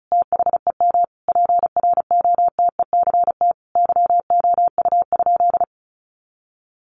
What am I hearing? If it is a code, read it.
Morse code, heard as a keyed tone, about 34 wpm, 700 Hz. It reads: THEM PROTECT YOU?
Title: Sentences from Top 900 Words 35wpm